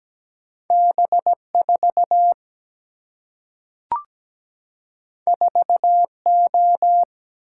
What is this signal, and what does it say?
Morse code, heard as a keyed tone, about 17 wpm, 700 Hz. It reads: B4 4O
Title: Single-Letter Single-Number 17wpm